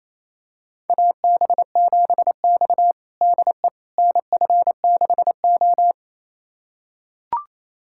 Morse code, heard as a keyed tone, about 28 wpm, 700 Hz. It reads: AB7X DE NF6O
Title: USA Callsign DE Callsign - 4 Characters - 28wpm